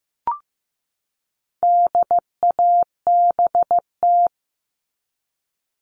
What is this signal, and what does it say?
Morse code, heard as a keyed tone, about 15 wpm, 700 Hz. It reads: DABT
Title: All Cut Numbers - 4-Digit 15wpm